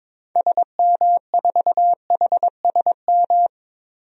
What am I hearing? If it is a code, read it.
Morse code, heard as a keyed tone, about 22 wpm, 700 Hz. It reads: SM4HSM
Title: DXCC-100 Call Signs - 6 Characters - 22wpm